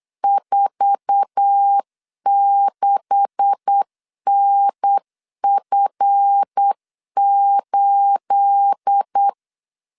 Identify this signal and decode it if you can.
Morse code, heard as a keyed tone, about 8 wpm, 800 Hz. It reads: 46NF8
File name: LEZIONI / 9 286 8wpm 3425